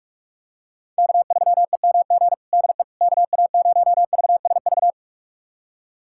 Morse code, heard as a keyed tone, about 46 wpm, 700 Hz. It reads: K3EMG DE KA0VSV
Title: USA Callsign DE Callsign 45wpm